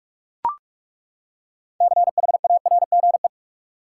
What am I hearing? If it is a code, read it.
Morse code, heard as a keyed tone, about 42 wpm, 700 Hz. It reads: CHARGE
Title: Six Letter Words - 45wpm